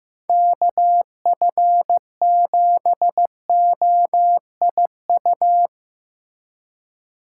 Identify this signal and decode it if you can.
Morse code, heard as a keyed tone, about 15 wpm, 700 Hz. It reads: KF7OIU
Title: USA Call Signs - 6 Characters - 4x Character Spacing - 15wpm